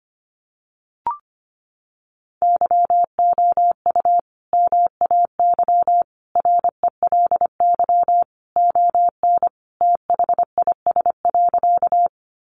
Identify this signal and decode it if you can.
Morse code, heard as a keyed tone, about 25 wpm, 700 Hz. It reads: YOU MAY RELY ON THIS.
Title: Sentences from Top 1200 Words 25wpm